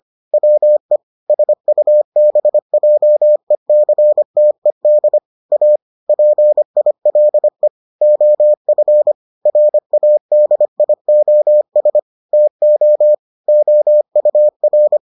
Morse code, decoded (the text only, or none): WE SUBJECTED A PILE OF RADIOS TO OUR